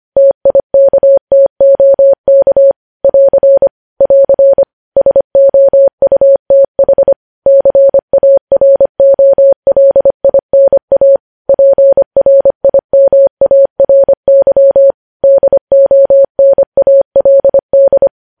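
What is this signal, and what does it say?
Morse code, heard as a keyed tone, about 25 wpm, 550 Hz. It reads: TIKTOK <AR> <AR> SOUTH CAROLINA PRIMARY DONALD
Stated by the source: Morse code practice of news headlines at 25 wpm, opening with vvv.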